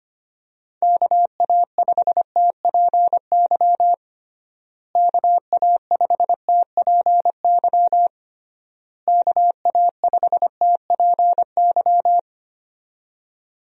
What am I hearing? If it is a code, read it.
Morse code, heard as a keyed tone, about 25 wpm, 700 Hz. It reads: KA5TPY KA5TPY KA5TPY
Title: USA Call Signs - 6 Characters - Repeated 3x - 25wpm